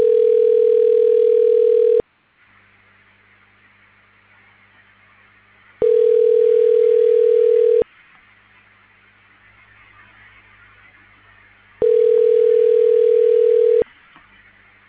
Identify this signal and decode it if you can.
Ringback tone (North America), 440 + 480 Hz.